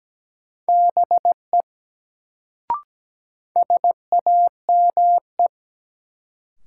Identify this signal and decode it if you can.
Morse code, heard as a keyed tone, about 17 wpm, 700 Hz. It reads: BE SAME
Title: Sets of 2 Words - New Word Drill - 17wpm